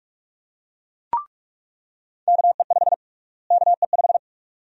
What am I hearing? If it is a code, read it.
Morse code, heard as a keyed tone, about 45 wpm, 700 Hz. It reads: KE5 KE5
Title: USA Call Sign Prefixes - Repeated 2x - 45wpm